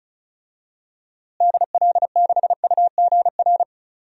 Morse code, heard as a keyed tone, about 36 wpm, 700 Hz. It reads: DL6UGR
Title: DXCC-100 Call Signs - 6 Characters - Repeated 3x - 35wpm